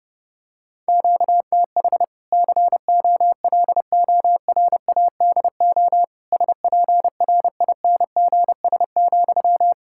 Morse code, heard as a keyed tone, about 30 wpm, 700 Hz. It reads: QTH COLORADO SPRINGS,
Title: Practice QSO - Exchange 1 - Beginning - 30wpm